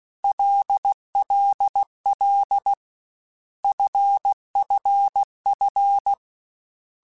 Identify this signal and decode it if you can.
Morse code, heard as a keyed tone, about 16 wpm, 800 Hz. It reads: LLL FFF
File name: Basic Letter Groups By7Groups / LFCYQ / LFCYQ@13